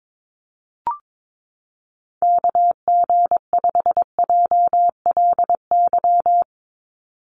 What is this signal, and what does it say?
Morse code, heard as a keyed tone, about 22 wpm, 700 Hz. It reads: KG5JLY
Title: USA Call Signs 22wpm